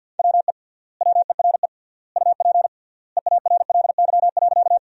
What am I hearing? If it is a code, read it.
Morse code, heard as a keyed tone, about 50 wpm, 700 Hz. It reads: WE WERE UP EARLY.
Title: 3x Word-Spacing - Sentences from Top 400 Words - 50wpm